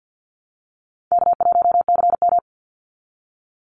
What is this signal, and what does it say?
Morse code, heard as a keyed tone, about 49 wpm, 700 Hz. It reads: K1CM
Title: NA SOTA Chaser Callsigns 50wpm